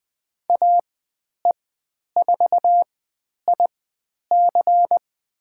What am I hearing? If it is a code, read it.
Morse code, heard as a keyed tone, about 20 wpm, spaced as for 10 wpm, 700 Hz. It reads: AE4IC
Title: 102 Lesson 17g - Letter u - Call Signs 20wpm